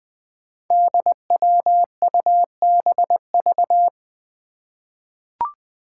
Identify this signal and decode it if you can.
Morse code, heard as a keyed tone, about 20 wpm, 700 Hz. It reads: DWUBV
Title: Five Challenging Letter-Number Combinations 20wpm